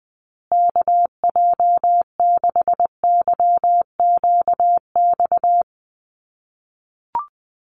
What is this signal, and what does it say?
Morse code, heard as a keyed tone, about 20 wpm, 700 Hz. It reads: KJ6YQX
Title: USA Call Signs Repeated 3x 20wpm